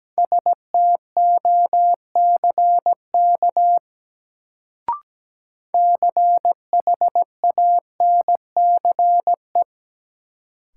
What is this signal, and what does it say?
Morse code, heard as a keyed tone, about 17 wpm, 700 Hz. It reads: STOCK CHANCE
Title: Sentences from Top 800 Words - New Word Drill - 17wpm